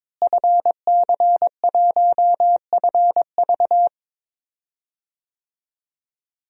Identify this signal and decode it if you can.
Morse code, heard as a keyed tone, about 22 wpm, 700 Hz. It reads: FC1FV
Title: DXCC-100 Call Signs - 5 Characters - Repeated 3x - 22wpm